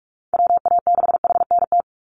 Morse code, heard as a keyed tone, about 44 wpm, 700 Hz. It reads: WA6HNT
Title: USA Call Signs 45wpm